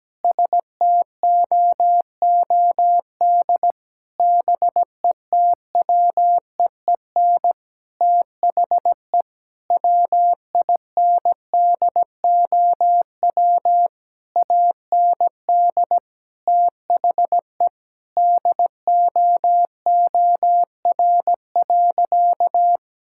Morse code, heard as a keyed tone, about 17 wpm, 700 Hz. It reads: STOOD BETWEEN THE WINDOW AND THE DOOR.
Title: Sentences from Top 1600 Words 17wpm